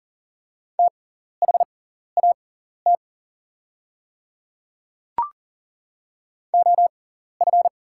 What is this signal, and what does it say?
Morse code, heard as a keyed tone, about 40 wpm, spaced as for 14 wpm, 700 Hz. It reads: THAT OF
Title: 6x Character Spacing - Top 100 Words - 40wpm